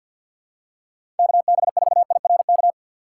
Morse code, heard as a keyed tone, about 46 wpm, 700 Hz. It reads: KB4IRK
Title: USA Call Signs - 6 Characters - 3x Character Spacing - 50wpm